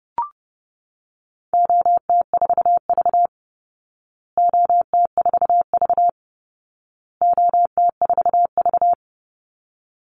Morse code, heard as a keyed tone, about 30 wpm, 700 Hz. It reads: OT4V OT4V OT4V
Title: Call Signs Repeated 2x - 30wpm